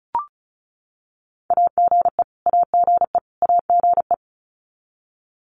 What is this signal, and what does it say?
Morse code, heard as a keyed tone, about 34 wpm, 700 Hz. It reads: AGE AGE AGE
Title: Top 500 Words Repeated 35wpm